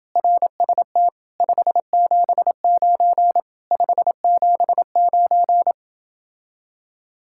Morse code, heard as a keyed tone, about 28 wpm, 700 Hz. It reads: RST 579 579